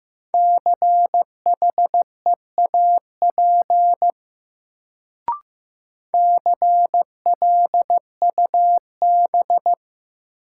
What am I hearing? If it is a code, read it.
Morse code, heard as a keyed tone, about 15 wpm, 700 Hz. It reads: CHEAP CLUB
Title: Sentences from Top 1200 Words - New Word Drill - 15wpm